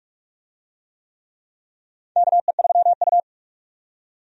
Morse code, heard as a keyed tone, about 42 wpm, 700 Hz. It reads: KE3U
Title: USA Call Signs - 4 Characters - 45wpm